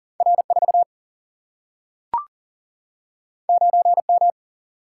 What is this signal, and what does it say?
Morse code, heard as a keyed tone, about 40 wpm, 700 Hz. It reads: R4 9M
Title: Single-Letter Single-Number 40wpm